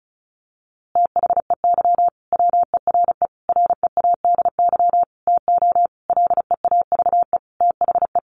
Morse code, heard as a keyed tone, about 34 wpm, 700 Hz. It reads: THEY WERE READY TO LEAVE THE
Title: A Sound Of Thunder 35wpm